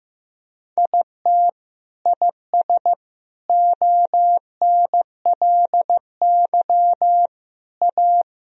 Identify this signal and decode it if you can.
Morse code, heard as a keyed tone, about 15 wpm, 700 Hz. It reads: IT IS ONLY A
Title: Sentences from Top 1000 Words - Review Part 2 - 15wpm